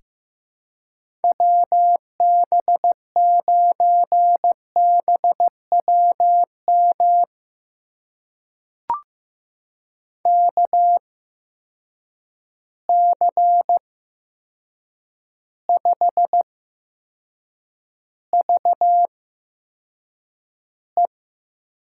Morse code, heard as a keyed tone, about 15 wpm, spaced as for 7 wpm, 700 Hz. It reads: WB9BWM KC5VE
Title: USA Call Signs - 6 Characters - 8x Character Spacing - 15wpm